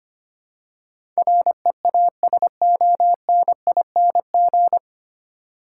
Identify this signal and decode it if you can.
Morse code, heard as a keyed tone, about 25 wpm, 700 Hz. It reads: REASONING
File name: Nine Letter Words - 25wpm